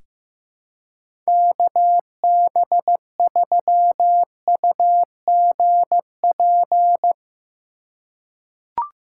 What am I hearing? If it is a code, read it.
Morse code, heard as a keyed tone, about 16 wpm, 700 Hz. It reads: KB3UGP
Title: USA Call Signs - 6 Characters - 2x Character Spacing - 15wpm